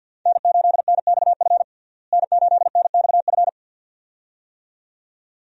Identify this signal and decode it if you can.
Morse code, heard as a keyed tone, about 51 wpm, 700 Hz. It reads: N8NXF N8NXF
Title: USA Call Signs - 5 Characters - Repeated 2x - 50wpm